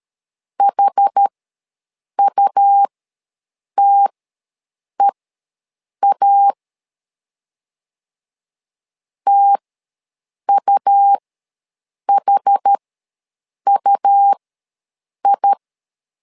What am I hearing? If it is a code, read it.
Morse code, heard as a keyed tone, about 13 wpm, spaced as for 7 wpm, 800 Hz. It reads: HUTEA TUHUI
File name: LEZIONI / 2 UHDN / 2 UH ws 12wpm 1730